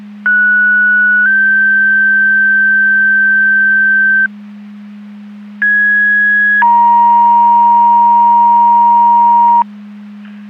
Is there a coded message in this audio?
Yes: two-tone page from 0.3 to 9.6 s, heard as a one-second tone, then a three-second tone, about 1470 then 1550 Hz, about 1650 then 940 Hz.